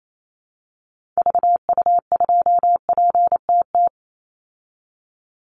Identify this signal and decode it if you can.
Morse code, heard as a keyed tone, about 28 wpm, 700 Hz. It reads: VU2PTT
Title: Call Signs Repeated 3x - 28wpm